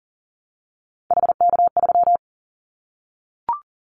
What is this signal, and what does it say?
Morse code, heard as a keyed tone, about 40 wpm, 700 Hz. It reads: HK3
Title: Callsign Prefixes for Top 100 DXCC - 40wpm